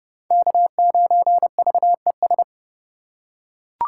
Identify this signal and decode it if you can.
Morse code, heard as a keyed tone, about 32 wpm, 700 Hz. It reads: K9VES
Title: USA Call Signs - 5 Characters - 6x Character Spacing - 30wpm